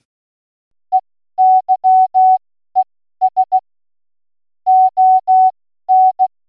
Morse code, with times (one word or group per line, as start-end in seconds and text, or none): 0.9-3.6 EYES
4.7-6.3 ON